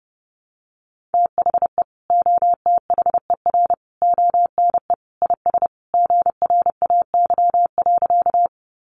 Morse code, heard as a keyed tone, about 30 wpm, 700 Hz. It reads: THE OTHER ONE IS GRAY.
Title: Additional Easy Phrases 30wpm